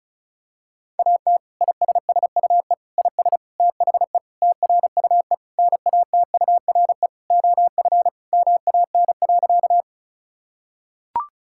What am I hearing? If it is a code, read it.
Morse code, heard as a keyed tone, about 36 wpm, 700 Hz. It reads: AT ISSUE IS THE TRUE NATURE OF MAN.
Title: Sentences from Top 1000 Words - Review Part 6 - 35wpm